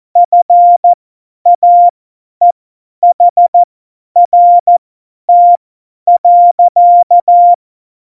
Morse code, heard as a keyed tone, about 14 wpm, spaced as for 10 wpm, 700 Hz. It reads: FAEHRT.
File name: Morsekurs / Witze / Witz 27. November 2018, 50 F70